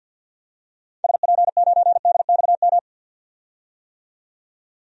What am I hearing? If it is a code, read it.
Morse code, heard as a keyed tone, about 50 wpm, 700 Hz. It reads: SP9DXM
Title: DXCC-100 Call Signs - 6 Characters - 50wpm